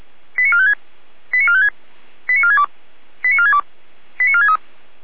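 Five-tone selective call (ZVEI), heard as quick runs of five tones, tones of about 70 ms. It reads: 89456 89456 89462 89462 89463
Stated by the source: ZVEI five-tone call received by SDR.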